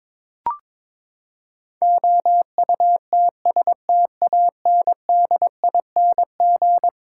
Morse code, heard as a keyed tone, about 21 wpm, 700 Hz. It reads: OUTSTANDING
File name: practice / 22wpm / Eleven Letter Words - 22wpm